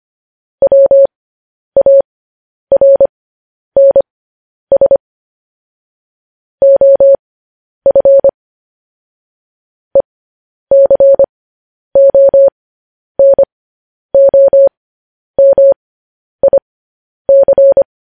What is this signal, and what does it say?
Morse code, heard as a keyed tone, about 25 wpm, spaced as for 10 wpm, 550 Hz. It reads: WARNS OF ECONOMIC
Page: news headlines in morse code at 10 wpm